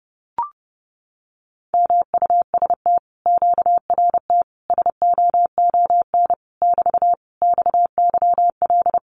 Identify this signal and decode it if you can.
Morse code, heard as a keyed tone, about 30 wpm, 700 Hz. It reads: MUST QRT SOON <BT> XYL
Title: Common QSO Phrases Encore 30wpm